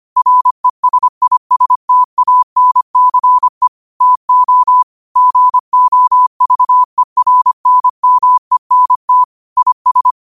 Morse code, heard as a keyed tone, about 25 wpm, 1000 Hz. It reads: RESISTANCE TO GOVERNMENT IS
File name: qotdfiles / 25wpm